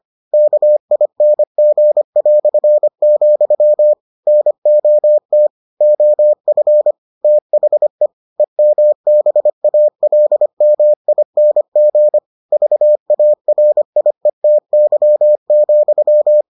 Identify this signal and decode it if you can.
Morse code, heard as a keyed tone, about 25 wpm, 600 Hz. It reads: KING", NOT OF THE EMBALMING VARIETY,